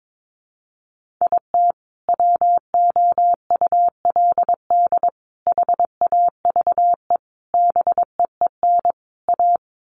Morse code, heard as a keyed tone, about 22 wpm, 700 Hz. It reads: IT WOULD HAVE BEEN A
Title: Sets of 5 Words Encore 22wpm